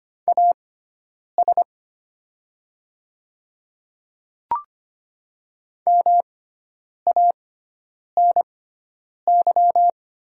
Morse code, heard as a keyed tone, about 25 wpm, spaced as for 9 wpm, 700 Hz. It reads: AS MANY
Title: 6x Character Spacing - Top 500 Words - 25wpm